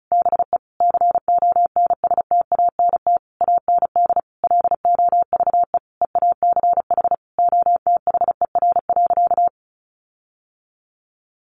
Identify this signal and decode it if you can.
Morse code, heard as a keyed tone, about 34 wpm, 700 Hz. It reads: BE CONSTANT AND LOVE EACH OTHER.
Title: Sentences from Top 1500 Words 35wpm